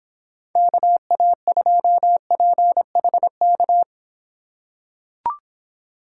Morse code, heard as a keyed tone, about 26 wpm, 700 Hz. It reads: KA2PHK